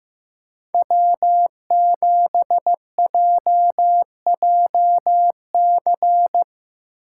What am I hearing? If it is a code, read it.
Morse code, heard as a keyed tone, about 17 wpm, 700 Hz. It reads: W7JJC
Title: Speed Racing - USA 1x3 Call Signs 15wpm